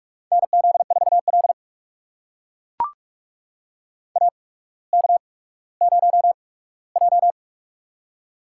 Morse code, heard as a keyed tone, about 44 wpm, spaced as for 19 wpm, 700 Hz. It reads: NZ4L AK0J
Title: USA Call Signs - 4 Characters - 8x Character Spacing - 45wpm